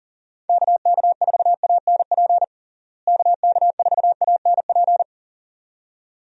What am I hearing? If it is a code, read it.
Morse code, heard as a keyed tone, about 37 wpm, 700 Hz. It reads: KK4ANP KK4ANP
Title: USA Call Signs - 6 Characters - Repeated 2x - 40wpm